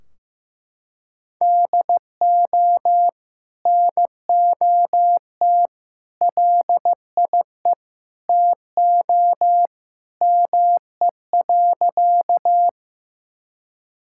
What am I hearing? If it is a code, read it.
Morse code, heard as a keyed tone, about 14 wpm, 700 Hz. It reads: DO NOT LIE TO ME.
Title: Sentences from Top 1600 Words 15wpm